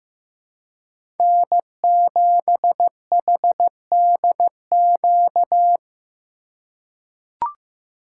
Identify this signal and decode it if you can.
Morse code, heard as a keyed tone, about 15 wpm, 700 Hz. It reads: N7HDQ
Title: USA Call Signs - 5 Characters - 3x Character Spacing - 15wpm